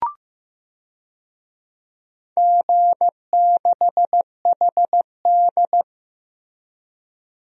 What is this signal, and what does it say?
Morse code, heard as a keyed tone, about 15 wpm, 700 Hz. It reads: G6HD
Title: DXCC-100 Call Signs - 4 Characters - Repeated 2x - 15wpm